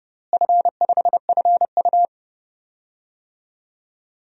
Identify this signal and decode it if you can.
Morse code, heard as a keyed tone, about 30 wpm, 700 Hz. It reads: F5FU
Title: DXCC-100 Call Signs - 4 Characters - Repeated 3x - 30wpm